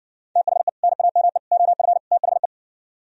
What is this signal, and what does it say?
Morse code, heard as a keyed tone, about 69 wpm, 700 Hz. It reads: THE NAME OF THE